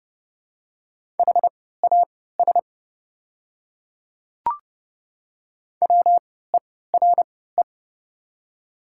Morse code, heard as a keyed tone, about 29 wpm, spaced as for 17 wpm, 700 Hz. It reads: HAS WERE
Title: 3x Character Spacing - Top 300 Words - 30wpm